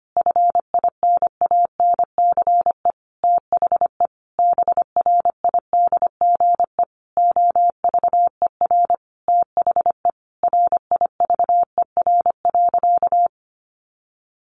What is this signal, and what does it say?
Morse code, heard as a keyed tone, about 25 wpm, 700 Hz. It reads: FINANCE THE BRIDGE OVER THE RIVER.
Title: Sentences from Top 1500 Words 25wpm